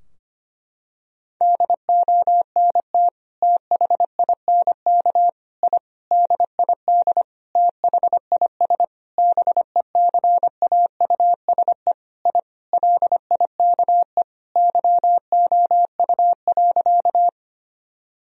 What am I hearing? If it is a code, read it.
Morse code, heard as a keyed tone, about 25 wpm, 700 Hz. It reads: DONT THINK I DID THIS BECAUSE I LIKE YOU.
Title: Sentences from Top 1000 Words - Review Part 3 - 25wpm